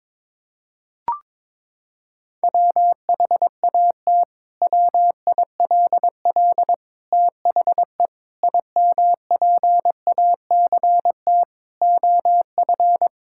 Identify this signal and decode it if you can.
Morse code, heard as a keyed tone, about 22 wpm, 700 Hz. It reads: WHAT WILL THE IMPACT OF
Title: Sentences from Top 1000 Words - Review Part 6 - 22wpm